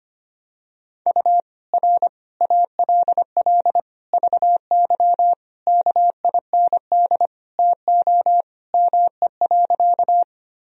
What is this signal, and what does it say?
Morse code, heard as a keyed tone, about 25 wpm, 700 Hz. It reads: U R ALL VY KIND TO ME.
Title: Sentences from Top 200 Words and Abbreviations 25wpm